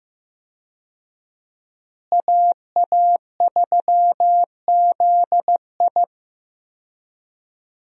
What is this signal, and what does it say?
Morse code, heard as a keyed tone, about 15 wpm, 700 Hz. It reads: AA3ZI
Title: USA Call Signs - 5 Characters - Repeated 3x - 15wpm